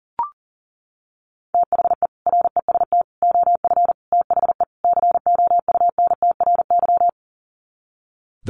Morse code, heard as a keyed tone, about 40 wpm, 700 Hz. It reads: THE REST OF THE COUNTRY
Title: Sets of 5 Words 40wpm